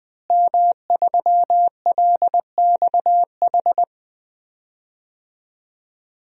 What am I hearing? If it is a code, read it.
Morse code, heard as a keyed tone, about 20 wpm, 700 Hz. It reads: M3LXH
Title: DXCC-100 Call Signs - 5 Characters - Repeated 2x - 20wpm